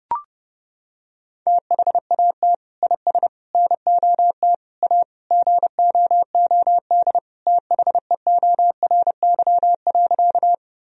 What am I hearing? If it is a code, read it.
Morse code, heard as a keyed tone, about 30 wpm, 700 Hz. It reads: THAT IS NOT A GOOD THEORY.
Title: Sentences from Top 1000 Words - Review Part 3 - 30wpm